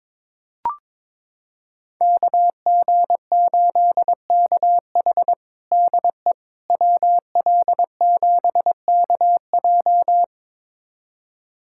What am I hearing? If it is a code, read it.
Morse code, heard as a keyed tone, about 22 wpm, 700 Hz. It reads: KG8KH DE WL7KJ
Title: USA Callsign DE Callsign - 5 Characters - 22wpm